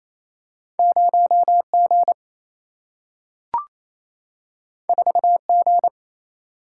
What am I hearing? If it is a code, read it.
Morse code, heard as a keyed tone, about 28 wpm, 700 Hz. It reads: 0G 4G
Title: Single-Letter Single-Number 28wpm